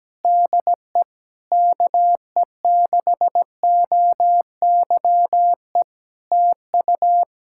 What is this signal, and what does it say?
Morse code, heard as a keyed tone, about 17 wpm, 700 Hz. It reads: DE KE6OYE TU
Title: Practice QSO - Exchange 4 - Ending - 17wpm